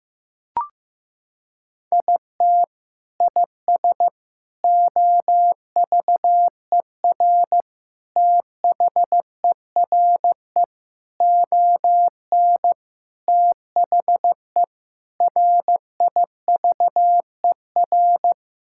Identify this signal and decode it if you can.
Morse code, heard as a keyed tone, about 15 wpm, 700 Hz. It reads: IT IS OVER THERE ON THE RIVER
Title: Sentences from Top 1000 Words - Review Part 6 - 15wpm